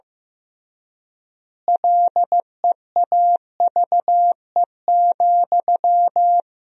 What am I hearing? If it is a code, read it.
Morse code, heard as a keyed tone, about 16 wpm, 700 Hz. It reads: LEAVE,
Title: 3x Word-Spacing - Sentences from Top 400 Words - 15wpm